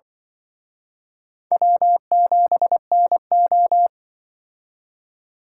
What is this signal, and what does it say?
Morse code, heard as a keyed tone, about 25 wpm, 700 Hz. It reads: W7NO
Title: Speed Racing - USA 1x2 Call Signs 20wpm